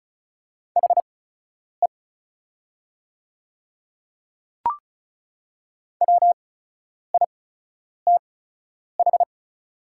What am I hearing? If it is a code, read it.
Morse code, heard as a keyed tone, about 36 wpm, spaced as for 10 wpm, 700 Hz. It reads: HE WITH
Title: 8x Character Spacing - Top 100 Words - 35wpm